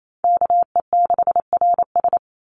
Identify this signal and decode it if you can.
Morse code, heard as a keyed tone, about 28 wpm, 700 Hz. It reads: KE6RS
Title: NA SOTA Chaser Callsigns 28wpm